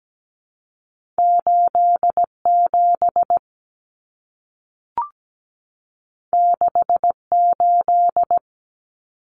Morse code, heard as a keyed tone, about 17 wpm, 700 Hz. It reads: 87 68
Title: Numbers - 2-Digit 17wpm